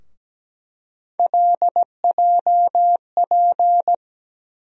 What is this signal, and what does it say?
Morse code, heard as a keyed tone, about 17 wpm, 700 Hz. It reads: LJP
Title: USA Call Sign Postfixes - 3 Letters 17wpm